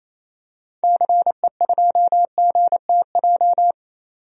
Morse code, heard as a keyed tone, about 28 wpm, 700 Hz. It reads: CE2GTJ
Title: DXCC-100 Call Signs - 6 Characters - 28wpm